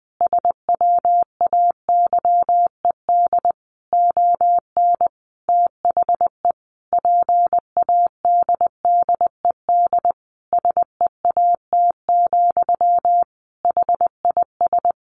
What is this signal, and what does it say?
Morse code, heard as a keyed tone, about 20 wpm, 700 Hz. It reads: SWAYED ON THE PADDED SEAT, HIS
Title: A Sound Of Thunder 20wpm